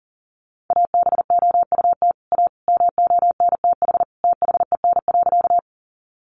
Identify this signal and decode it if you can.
Morse code, heard as a keyed tone, about 38 wpm, 700 Hz. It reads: ABOUT A MONTH THEN.
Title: Sentences from Top 800 Words 40wpm